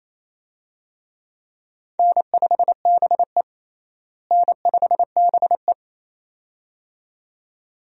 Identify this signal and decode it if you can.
Morse code, heard as a keyed tone, about 28 wpm, 700 Hz. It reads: N5BE N5BE
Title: USA Call Signs - 4 Characters - Repeated 2x - 28wpm